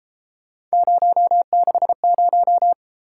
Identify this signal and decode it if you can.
Morse code, heard as a keyed tone, about 32 wpm, 700 Hz. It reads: 060